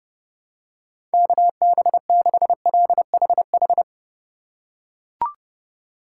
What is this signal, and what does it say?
Morse code, heard as a keyed tone, about 30 wpm, 700 Hz. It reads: KB6LHH
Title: USA Call Signs - 6 Characters - 2x Character Spacing - 30wpm